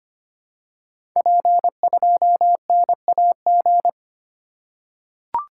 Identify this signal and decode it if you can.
Morse code, heard as a keyed tone, about 25 wpm, 700 Hz. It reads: P2NAG
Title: Code Groups - 5 Letter:Numbers - 25wpm